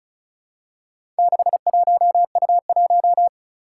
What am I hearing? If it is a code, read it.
Morse code, heard as a keyed tone, about 36 wpm, 700 Hz. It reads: 61U1
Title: Four Challenging Letter-Number Combinations 35wpm